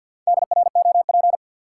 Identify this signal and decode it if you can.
Morse code, heard as a keyed tone, about 46 wpm, 700 Hz. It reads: DROP
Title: Sentences from Top 1400 Words - New Word Drill Alt - 50wpm